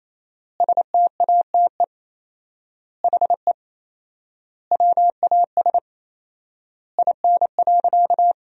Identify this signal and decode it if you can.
Morse code, heard as a keyed tone, about 28 wpm, 700 Hz. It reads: STATE HE WAS IN.
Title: 4x Word-Spacing - Sentences from Top 400 Words - 28wpm